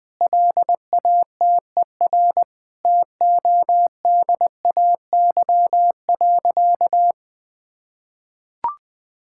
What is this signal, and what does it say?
Morse code, heard as a keyed tone, about 20 wpm, 700 Hz. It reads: LATER TODAY.
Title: Sentences with 4-Word Phrases - Spoken Beginning - 20wpm